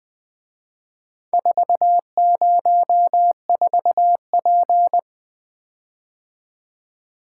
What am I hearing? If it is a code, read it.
Morse code, heard as a keyed tone, about 24 wpm, 700 Hz. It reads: 404P
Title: Speed-Racing - Four Letter-Number Combinations - 20wpm